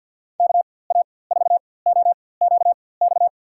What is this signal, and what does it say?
Morse code, heard as a keyed tone, about 48 wpm, spaced as for 23 wpm, 700 Hz. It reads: KA4OQX
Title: USA Call Signs - 6 Characters - 4x Character Spacing - 50wpm